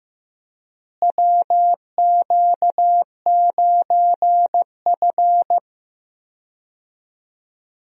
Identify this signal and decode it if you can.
Morse code, heard as a keyed tone, about 15 wpm, 700 Hz. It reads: WQ9F
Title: USA Call Signs - 4 Characters - Repeated 2x - 15wpm